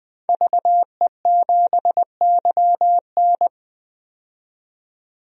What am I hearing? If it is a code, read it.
Morse code, heard as a keyed tone, about 21 wpm, 700 Hz. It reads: VE7YN
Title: DXCC-100 Call Signs - 5 Characters - 20wpm